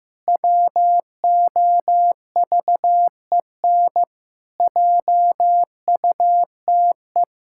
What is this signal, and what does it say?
Morse code, heard as a keyed tone, about 15 wpm, 700 Hz. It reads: WOVEN JUTE
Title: Pangram Challenge v2 - 15wpm